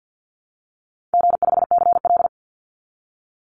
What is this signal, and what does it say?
Morse code, heard as a keyed tone, about 50 wpm, 700 Hz. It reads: G5CL
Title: Call Signs Repeated 2x - 50wpm